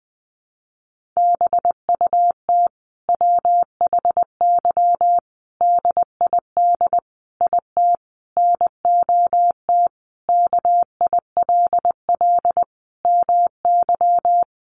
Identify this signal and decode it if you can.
Morse code, heard as a keyed tone, about 20 wpm, 700 Hz. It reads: BUT WHY DID IT NOT KILL MY
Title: Sentences from Top 1400 Words 20wpm